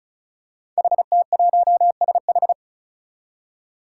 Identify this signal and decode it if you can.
Morse code, heard as a keyed tone, about 36 wpm, 700 Hz. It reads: HT1SH
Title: Five Challenging Letter-Number Combinations 35wpm